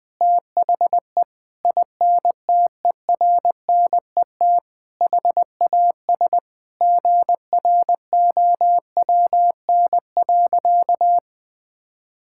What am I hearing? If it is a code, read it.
Morse code, heard as a keyed tone, about 20 wpm, 700 Hz. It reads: THE INTERNET HAS GROWN.
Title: Sentences with 4-Word Phrases - 20wpm